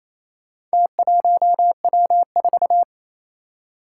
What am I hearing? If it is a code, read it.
Morse code, heard as a keyed tone, about 27 wpm, 700 Hz. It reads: T1W4
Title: Four Challenging Letter-Number Combinations 28wpm